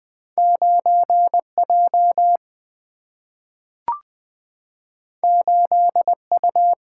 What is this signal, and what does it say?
Morse code, heard as a keyed tone, about 20 wpm, 700 Hz. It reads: 9 J 8 U
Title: Single-Letter Single-Number 20wpm